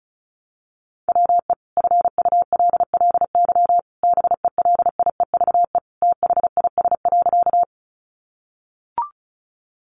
Morse code, heard as a keyed tone, about 34 wpm, 700 Hz. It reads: WE FULLY BELIEVE THIS.
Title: Sentences from Top 1300 Words 35wpm